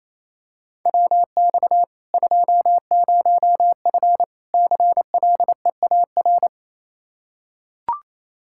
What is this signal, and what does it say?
Morse code, heard as a keyed tone, about 28 wpm, 700 Hz. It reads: WX 20F CLEAR
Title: K1USN SST QSO Practice 28wpm